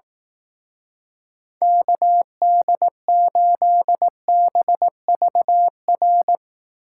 Morse code, heard as a keyed tone, about 18 wpm, 700 Hz. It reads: KD8BVR
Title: Speed Racing - USA 2x3 Call Signs 15wpm